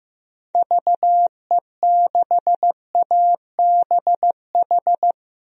Morse code, heard as a keyed tone, about 15 wpm, 700 Hz. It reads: VE6ABH
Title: DXCC-100 Call Signs - 6 Characters - 15wpm